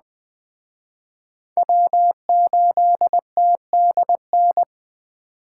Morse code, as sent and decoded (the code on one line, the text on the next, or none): .-- ---.. - -.. -.
W8TDN